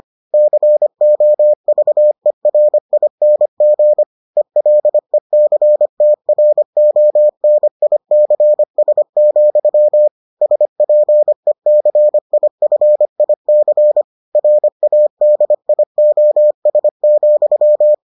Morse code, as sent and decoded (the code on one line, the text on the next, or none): -.-. --- ...- . .-. .. -. --. / . .-.. -.-. - .-. --- -. .. -.-. ... --..-- / ... .--. -.-. .. ..-. .. -.-. / .-. .- -.. .. --- ... --..--
COVERING ELECTRONICS, SPECIFIC RADIOS,